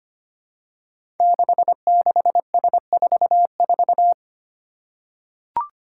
Morse code, heard as a keyed tone, about 25 wpm, 700 Hz. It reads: 66S44
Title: Five Challenging Letter-Number Combinations 25wpm